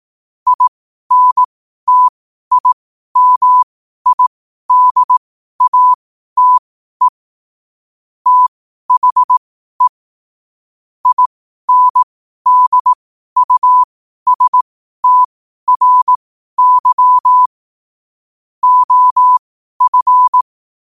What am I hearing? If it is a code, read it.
Morse code, heard as a keyed tone, about 18 wpm, spaced as for 12 wpm, 1000 Hz. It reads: INTIMIDATE THE INDUSTRY OF
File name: qotdfiles / 12wpm